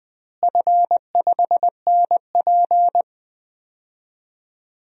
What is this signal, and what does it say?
Morse code, heard as a keyed tone, about 20 wpm, 700 Hz. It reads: F5NP
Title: DXCC-100 Call Signs - 4 Characters - Repeated 3x - 20wpm